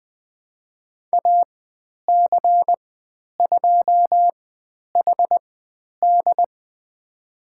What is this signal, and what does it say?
Morse code, heard as a keyed tone, about 20 wpm, spaced as for 10 wpm, 700 Hz. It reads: AC2HD
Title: 095 Lesson 16g - Letter c - Call Signs 20wpm